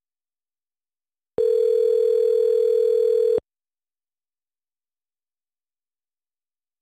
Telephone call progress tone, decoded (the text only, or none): ringback tone